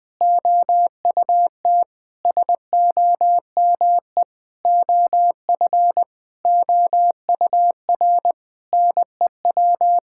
Morse code, heard as a keyed tone, about 20 wpm, 700 Hz. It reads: OUT SOME OF OUR NEW
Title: Sentences with 9-Word Phrases - Spoken Beginning - 20wpm